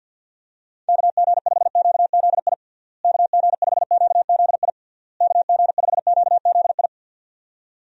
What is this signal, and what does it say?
Morse code, heard as a keyed tone, about 49 wpm, 700 Hz. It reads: KG5QZI KG5QZI KG5QZI
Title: USA Call Signs - 6 Characters - Repeated 3x - 50wpm